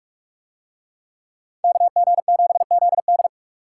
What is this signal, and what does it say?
Morse code, heard as a keyed tone, about 44 wpm, 700 Hz. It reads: KG7ZD